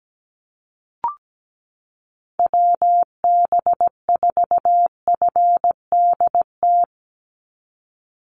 Morse code, heard as a keyed tone, about 17 wpm, 700 Hz. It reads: WB4FDT
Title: Call Signs Repeated 2x - 17wpm